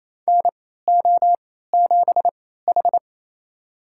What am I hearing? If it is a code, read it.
Morse code, heard as a keyed tone, about 28 wpm, spaced as for 16 wpm, 700 Hz. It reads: NO7H
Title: USA Call Signs - 4 Characters - 3x Character Spacing - 28wpm